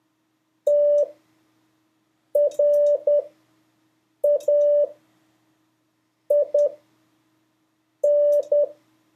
Morse code, heard as a keyed tone, about 10 wpm, spaced as for 5 wpm, 580 Hz. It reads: TRAIN
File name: 2of5-words-1